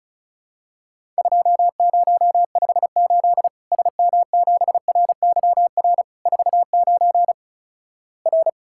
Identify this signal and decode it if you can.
Morse code, heard as a keyed tone, about 36 wpm, 700 Hz. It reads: 2058 SM7RYR 49 R
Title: Worked All Europe DX Contest - QTC Exchange - 35wpm